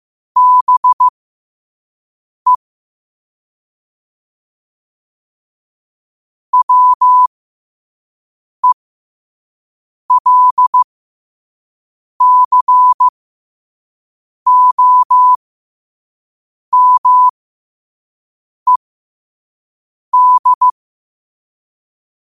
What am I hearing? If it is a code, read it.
Morse code, heard as a keyed tone, about 15 wpm, spaced as for 5 wpm, 1000 Hz. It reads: BE WELCOMED
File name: qotdfiles / 5wpm